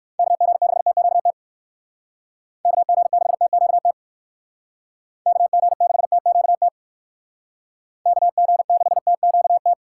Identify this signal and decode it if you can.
Morse code, heard as a keyed tone, about 58 wpm, 700 Hz. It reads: KG6TQT KG6TQT KG6TQT KG6TQT